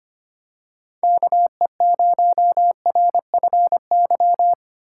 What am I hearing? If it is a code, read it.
Morse code, heard as a keyed tone, about 25 wpm, 700 Hz. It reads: KE0RFY
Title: USA Call Signs - 6 Characters - 4x Character Spacing - 25wpm